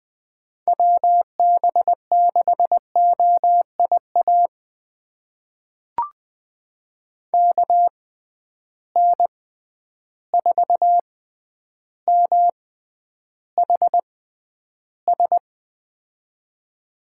Morse code, heard as a keyed tone, about 20 wpm, spaced as for 10 wpm, 700 Hz. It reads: WB6OIA KN4MHS
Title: USA Call Signs - 6 Characters - 6x Character Spacing - 20wpm